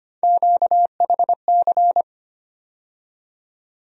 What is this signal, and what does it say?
Morse code, heard as a keyed tone, about 25 wpm, 700 Hz. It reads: QHC
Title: USA Call Sign Postfixes - 3 Letters 25wpm